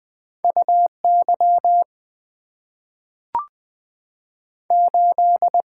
Morse code, heard as a keyed tone, about 20 wpm, 690 Hz. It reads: U Y 8
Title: Two Letter-Number Combinations - 20wpm